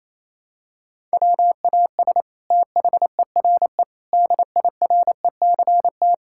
Morse code, heard as a keyed tone, about 27 wpm, 700 Hz. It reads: WAS THERE DIRECT
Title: Sentences from Top 1000 Words - Review Part 6 - 28wpm